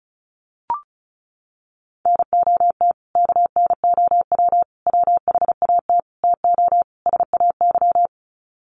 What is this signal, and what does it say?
Morse code, heard as a keyed tone, about 34 wpm, 700 Hz. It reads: NOT KNOW WHAT TO SAY
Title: Sets of 5 Words 35wpm